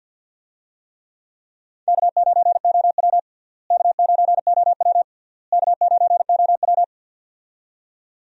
Morse code, heard as a keyed tone, about 48 wpm, 700 Hz. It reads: K9OW K9OW K9OW
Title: USA Call Signs - 4 Characters - Repeated 3x - 50wpm